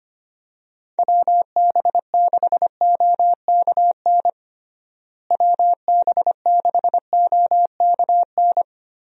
Morse code, heard as a keyed tone, about 25 wpm, 700 Hz. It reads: WB6OKN WB6OKN
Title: USA Call Signs - 6 Characters - Repeated 3x - 25wpm